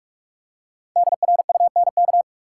Morse code, heard as a keyed tone, about 43 wpm, 700 Hz. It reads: DRUNK
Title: Five Letter Words - 45wpm